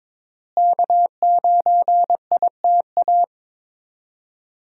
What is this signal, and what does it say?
Morse code, heard as a keyed tone, about 22 wpm, 700 Hz. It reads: K9ITA
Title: USA Call Signs - 5 Characters - Repeated 2x - 22wpm